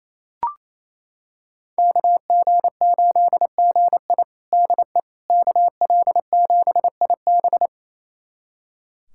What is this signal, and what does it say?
Morse code, heard as a keyed tone, about 28 wpm, 700 Hz. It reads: KG8GI DE KL7IB
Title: USA Callsign DE Callsign 28wpm